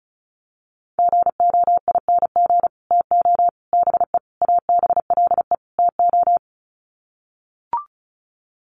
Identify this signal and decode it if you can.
Morse code, heard as a keyed tone, about 34 wpm, 700 Hz. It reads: GOING TO BE ABLE TO
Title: Sets of 5 Words 35wpm